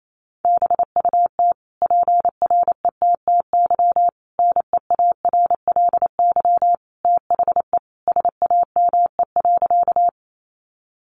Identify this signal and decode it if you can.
Morse code, heard as a keyed tone, about 28 wpm, 700 Hz. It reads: BUT PRETTY NEARLY THE SAME.
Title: Sentences from Top 1300 Words 28wpm